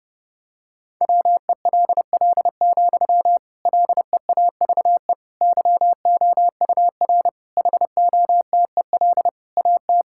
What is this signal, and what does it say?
Morse code, heard as a keyed tone, about 30 wpm, 700 Hz. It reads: WELL, LEAVE YOUR HOTEL AT
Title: Sentences from Top 1600 Words 30wpm